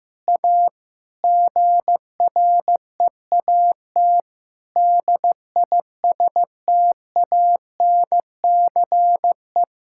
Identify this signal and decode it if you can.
Morse code, heard as a keyed tone, about 15 wpm, 700 Hz. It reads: A GREAT DISTANCE
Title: Sentences from Top 1600 Words 15wpm